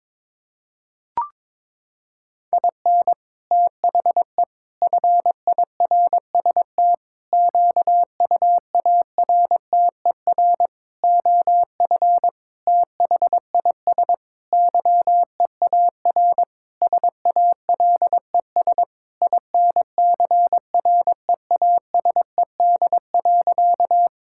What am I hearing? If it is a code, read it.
Morse code, heard as a keyed tone, about 22 wpm, 700 Hz. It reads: IN THE FIRST QUARTER OF THIS YEAR SALES INCREASED.
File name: Sentences with 7-Word Phrases - 22wpm